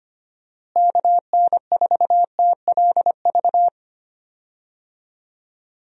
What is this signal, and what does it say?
Morse code, heard as a keyed tone, about 25 wpm, 700 Hz. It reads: KN4TLV
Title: USA Call Signs - 6 Characters - Repeated 2x - 25wpm